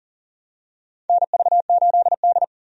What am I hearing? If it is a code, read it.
Morse code, heard as a keyed tone, about 40 wpm, 700 Hz. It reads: NV8D
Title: USA Call Signs - 4 Characters - 40wpm